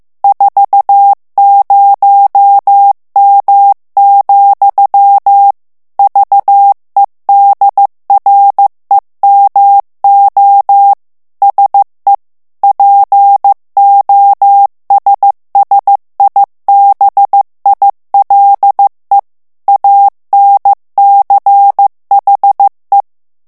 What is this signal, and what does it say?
Morse code, heard as a keyed tone, about 15 wpm, 800 Hz. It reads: 40M, VEDREMO SE POSSIBILE ANCHE
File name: ITALIEN texte clair 1 vitesse 14 mots minute